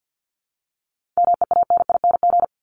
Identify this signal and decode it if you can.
Morse code, heard as a keyed tone, about 50 wpm, 700 Hz. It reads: MEANING
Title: Top 751-1000 Words 50wpm